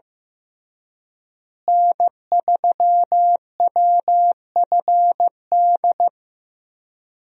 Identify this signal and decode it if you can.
Morse code, heard as a keyed tone, about 15 wpm, 700 Hz. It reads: N3WFD
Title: USA Call Signs - 5 Characters - Repeated 2x - 15wpm